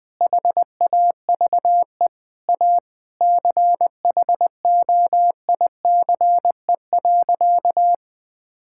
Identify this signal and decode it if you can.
Morse code, heard as a keyed tone, about 20 wpm, 700 Hz. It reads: HAVE A CHOICE.
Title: Sentences with 4-Word Phrases - Spoken Beginning - 20wpm